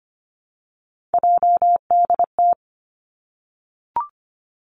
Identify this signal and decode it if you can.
Morse code, heard as a keyed tone, about 25 wpm, 700 Hz. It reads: JDT
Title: Three Challenging Letter-Number Combinations 25wpm